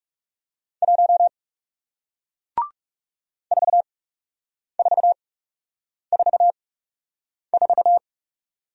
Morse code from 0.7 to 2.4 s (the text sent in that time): 1